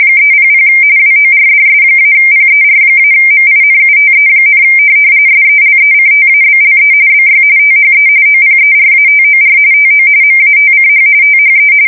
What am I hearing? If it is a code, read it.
RTTY, about 45.45 baud. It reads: 300 WORDS-PER-MINUTE, SELECTABLE BY THE OPERATOR TO MATCH THE
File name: fsk170 / teletype-corp